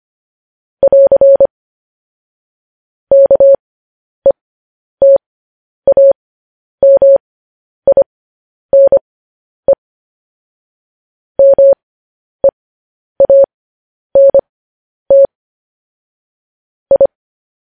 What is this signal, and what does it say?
Morse code, heard as a keyed tone, about 25 wpm, spaced as for 10 wpm, 550 Hz. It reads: <AR> KETAMINE MEANT I
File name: podcast / News headlines-1x-25x10